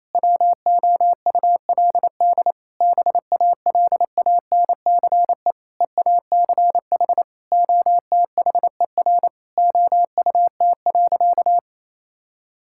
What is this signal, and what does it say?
Morse code, heard as a keyed tone, about 27 wpm, 700 Hz. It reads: WOULD BALANCE EACH OTHER OUT.
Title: Sentences from Top 1000 Words - Review Part 6 - 28wpm